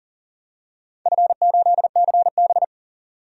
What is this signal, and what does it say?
Morse code, heard as a keyed tone, about 38 wpm, 700 Hz. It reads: F8CB
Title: DXCC-100 Call Signs - 4 Characters - 40wpm